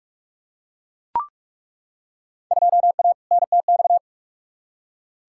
Morse code, heard as a keyed tone, about 41 wpm, 700 Hz. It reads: 2A NTX
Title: Field Day Exchanges - 45wpm